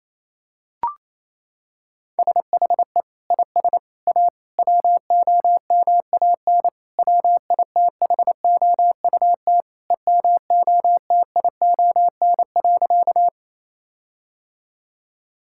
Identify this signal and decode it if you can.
Morse code, heard as a keyed tone, about 28 wpm, 700 Hz. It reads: SHE IS A WOMAN WITHOUT EMOTION.
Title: Sentences from Top 1600 Words 28wpm